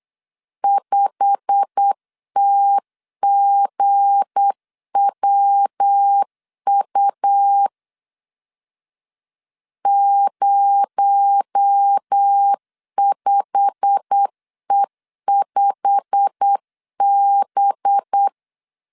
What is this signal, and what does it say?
Morse code, heard as a keyed tone, about 8 wpm, 800 Hz. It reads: 5TGWU 05E5B
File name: LEZIONI / 8 YJ45 / 8 YJ45 8wpm 3214